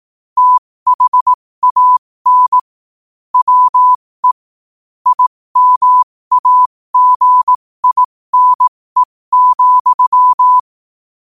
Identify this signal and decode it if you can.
Morse code, heard as a keyed tone, about 18 wpm, spaced as for 15 wpm, 1000 Hz. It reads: THAN WE IMAGINE,
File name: qotdfiles / 15wpm